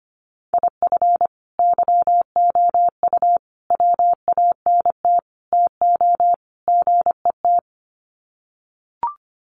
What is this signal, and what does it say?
Morse code, heard as a keyed tone, about 25 wpm, 700 Hz. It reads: IF YOU WANT TO GET
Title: Sets of 5 Words Encore 25wpm